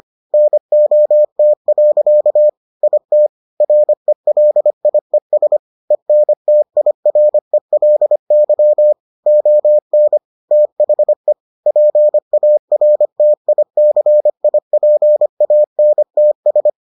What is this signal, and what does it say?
Morse code, heard as a keyed tone, about 25 wpm, 600 Hz. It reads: NOT. IT RELIES ENTIRELY ON THE PARTICIPANTS